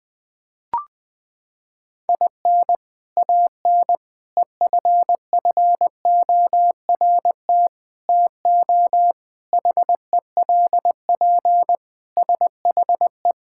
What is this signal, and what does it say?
Morse code, heard as a keyed tone, about 20 wpm, 700 Hz. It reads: IN AN EFFORT TO HELP SHE
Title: Sentences with 4-Word Phrases - 20wpm